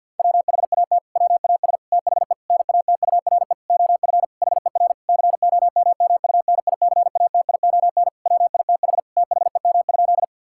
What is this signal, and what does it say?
Morse code, heard as a keyed tone, about 50 wpm, 700 Hz. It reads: WHAT WAS THE NATURE OF HER COMMUNICATION WITH THEM?
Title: Sentences from Top 1000 Words - Review Part 6 - 50wpm